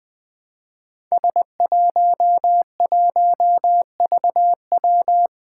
Morse code, heard as a keyed tone, about 20 wpm, 700 Hz. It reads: S11VW
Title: Five Challenging Letter-Number Combinations 20wpm